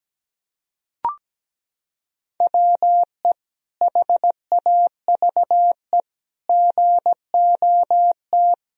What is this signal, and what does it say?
Morse code, heard as a keyed tone, about 17 wpm, 700 Hz. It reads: WE HAVE GOT
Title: Sentences from Top 1000 Words - Review Part 3 - 17wpm